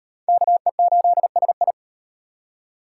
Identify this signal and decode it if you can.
Morse code, heard as a keyed tone, about 34 wpm, 700 Hz. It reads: KE8SI